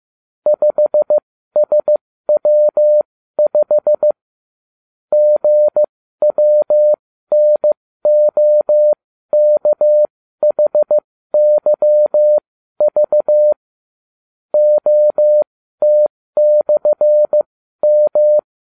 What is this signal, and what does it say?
Morse code, heard as a keyed tone, about 15 wpm, spaced as for 12 wpm, 600 Hz. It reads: 5SW5 GWNOKHYV OT/M